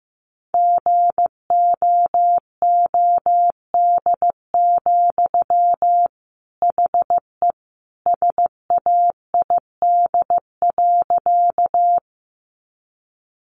Morse code, heard as a keyed tone, about 15 wpm, 700 Hz. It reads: GOOD, HE SAID.
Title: Sentences from Top 200 Words 15wpm